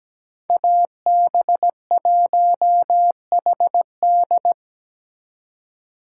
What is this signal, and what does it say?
Morse code, heard as a keyed tone, about 20 wpm, 700 Hz. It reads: AB1HD
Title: Speed Racing - USA 2x2 Call Signs 17wpm